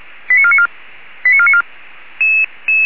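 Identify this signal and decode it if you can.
Five-tone selective call (ZVEI), heard as quick runs of five tones, tones of about 70 ms. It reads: 89494 89494
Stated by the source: ZVEI five-tone selective call 89200 received on an SDR.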